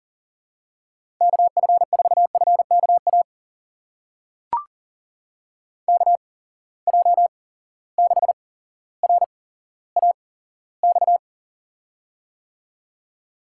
Morse code, heard as a keyed tone, about 40 wpm, spaced as for 17 wpm, 700 Hz. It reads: KF4FKA KJ6RAX
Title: USA Call Signs - 6 Characters - 8x Character Spacing - 40wpm